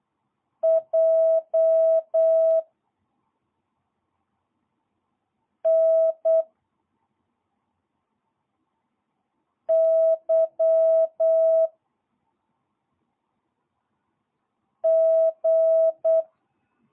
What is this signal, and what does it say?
Morse code, heard as a keyed tone, about 8 wpm, 660 Hz. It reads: J N Y G
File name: jh8cbh / 2022cw